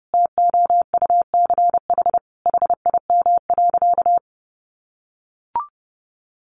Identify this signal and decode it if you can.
Morse code, heard as a keyed tone, about 30 wpm, 700 Hz. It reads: TOUCH HIM.
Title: Sentences from Top 1200 Words 30wpm